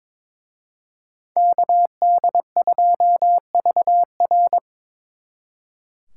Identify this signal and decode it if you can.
Morse code, heard as a keyed tone, about 26 wpm, 700 Hz. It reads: KD2VR
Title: Speed Racing - USA 2x2 Call Signs 22wpm